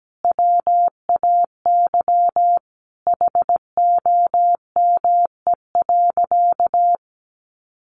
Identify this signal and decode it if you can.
Morse code, heard as a keyed tone, about 17 wpm, 700 Hz. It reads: WAY HOME.
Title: Sentences from Top 1300 Words 17wpm